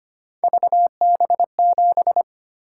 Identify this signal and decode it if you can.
Morse code, heard as a keyed tone, about 25 wpm, 700 Hz. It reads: VB7
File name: Three Letter-Number Combinations - 25wpm